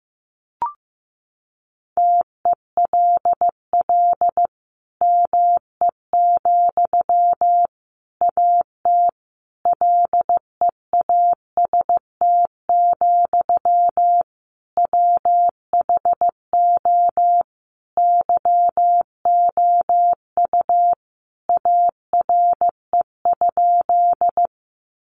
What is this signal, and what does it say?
Morse code, heard as a keyed tone, about 14 wpm, 700 Hz. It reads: TELL ME, AT LEAST, WHO YOU ARE?
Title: Sentences from Top 700 Words 15wpm